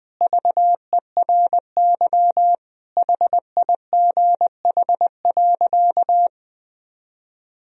Morse code, heard as a keyed tone, about 20 wpm, 700 Hz. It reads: VERY HIGH.
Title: Sentences from Top 1000 Words - Review Part 1 - 20wpm